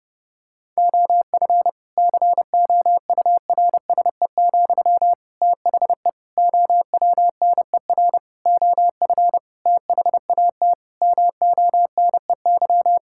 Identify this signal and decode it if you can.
Morse code, heard as a keyed tone, about 32 wpm, 700 Hz. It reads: OF COURSE, THE OWNER OF THAT MONEY
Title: Sentences from Top 1600 Words 30wpm